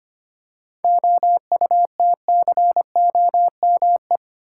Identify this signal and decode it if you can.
Morse code, heard as a keyed tone, about 25 wpm, 700 Hz. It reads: OUTCOME
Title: Seven Letter Words - 25wpm